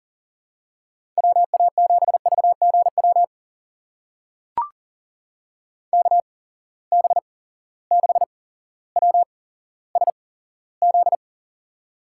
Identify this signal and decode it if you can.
Morse code, heard as a keyed tone, about 40 wpm, spaced as for 17 wpm, 700 Hz. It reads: WA7VGW KB6WSZ
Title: USA Call Signs - 8x Character Spacing - 40wpm